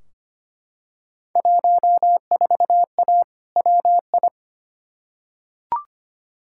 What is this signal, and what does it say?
Morse code, heard as a keyed tone, about 25 wpm, 700 Hz. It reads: 14A WI
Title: Field Day Exchanges Encore - 25wpm